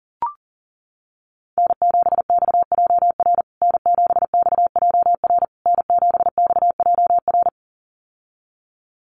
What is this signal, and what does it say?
Morse code, heard as a keyed tone, about 40 wpm, 700 Hz. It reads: N7XJR N7XJR N7XJR
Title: USA Call Signs Repeated 3x 40wpm